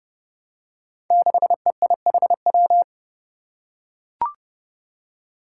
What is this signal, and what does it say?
Morse code, heard as a keyed tone, about 30 wpm, 700 Hz. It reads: 6EIHW
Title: Five Challenging Letter-Number Combinations 30wpm